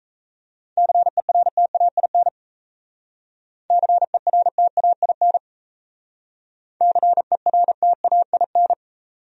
Morse code, heard as a keyed tone, about 35 wpm, 700 Hz. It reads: CERTAIN CERTAIN CERTAIN